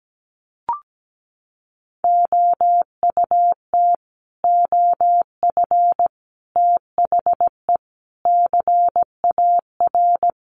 Morse code, heard as a keyed tone, about 17 wpm, 700 Hz. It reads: OUT OF THE CAR
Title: Sets of 5 Words Encore 17wpm